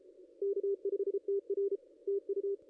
Morse code, heard as a keyed tone, about 34 wpm, 410 Hz, with radio noise. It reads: K5TR TU
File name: contests / ss-cw-k5tr-op-n5rz -0500